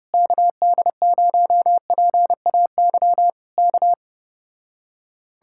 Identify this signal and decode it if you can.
Morse code, heard as a keyed tone, about 32 wpm, 700 Hz. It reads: KD0PAY K